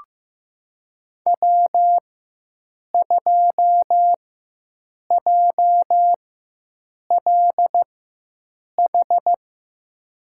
Morse code, heard as a keyed tone, about 15 wpm, spaced as for 7 wpm, 700 Hz. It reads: W2JLH
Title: USA Call Signs - 5 Characters - 4x Character Spacing - 15wpm